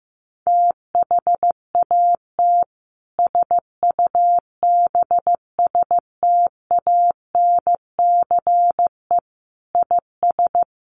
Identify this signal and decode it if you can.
Morse code, heard as a keyed tone, about 14 wpm, 700 Hz. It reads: THAT SUBSTANCE IS
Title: Sentences from Top 1500 Words 15wpm